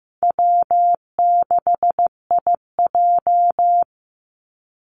Morse code, heard as a keyed tone, about 15 wpm, 700 Hz. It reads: W6IJ
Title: Call Signs Repeated 2x - 15wpm